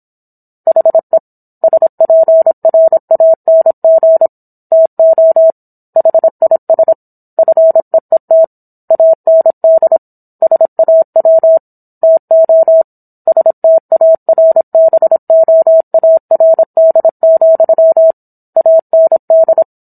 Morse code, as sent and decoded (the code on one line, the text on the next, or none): .... / ... .--. .-. .- -. --. / - --- / .... .. ... / ..-. . - / .- -. -.. / ... .- .-- / - --- / ... - .- .-. -... --- .- .-. -.. --..-- / .- -. -..
HE SPRANG TO HIS FEET AND SAW TO STARBOARD, AND